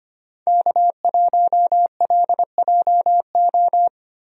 Morse code, heard as a keyed tone, about 25 wpm, 700 Hz. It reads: K1LJO
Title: USA Call Signs - 5 Characters - Repeated 2x - 25wpm